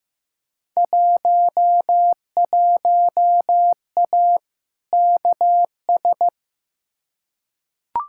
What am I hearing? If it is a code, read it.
Morse code, heard as a keyed tone, about 15 wpm, 700 Hz. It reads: 11A KS
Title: Field Day Exchanges Encore - 15wpm